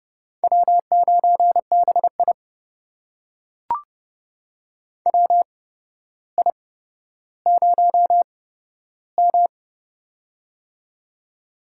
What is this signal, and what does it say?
Morse code, heard as a keyed tone, about 30 wpm, spaced as for 13 wpm, 700 Hz. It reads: W9BI WI0M
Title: USA Call Signs - 4 Characters - 8x Character Spacing - 30wpm